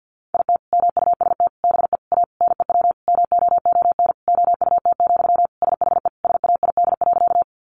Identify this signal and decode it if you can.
Morse code, heard as a keyed tone, about 50 wpm, 700 Hz. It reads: IT MUST BE A NEW MOON OUT, SHE SAID.
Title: Sentences from Top 500 Words 50wpm